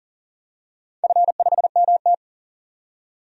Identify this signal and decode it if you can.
Morse code, heard as a keyed tone, about 41 wpm, 700 Hz. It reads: F5MT
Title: DXCC-100 Call Signs - 4 Characters - 40wpm